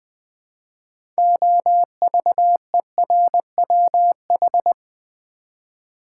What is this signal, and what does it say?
Morse code, heard as a keyed tone, about 20 wpm, 700 Hz. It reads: OVERWH
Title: Build A Word - 6 Letters At A Time - 12-Letter Words - 20wpm